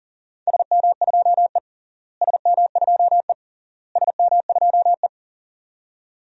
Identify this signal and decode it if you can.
Morse code, heard as a keyed tone, about 40 wpm, 700 Hz. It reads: SM2E SM2E SM2E
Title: DXCC-100 Call Signs - 4 Characters - Repeated 3x - 40wpm